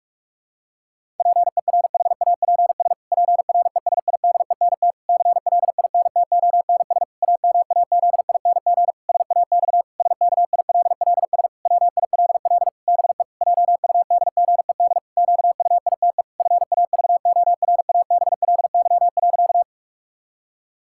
Morse code, heard as a keyed tone, about 46 wpm, 700 Hz. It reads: PERHAPS PRESIDENT CLINTONS AMAZING SAX SKILLS WILL BE JUDGED QUITE FAVORABLY.
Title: Pangram Challenge v2 - 45wpm